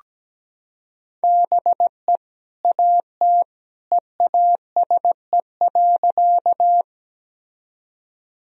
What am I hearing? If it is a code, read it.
Morse code, heard as a keyed tone, about 17 wpm, 700 Hz. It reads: BE AT EASE.
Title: Sentences from Top 1000 Words - Review Part 1 - 17wpm